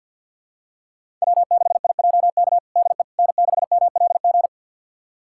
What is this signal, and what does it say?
Morse code, heard as a keyed tone, about 51 wpm, 700 Hz. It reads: W6IJK DE N6MLG
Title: USA Callsign DE Callsign - 5 Characters - 50wpm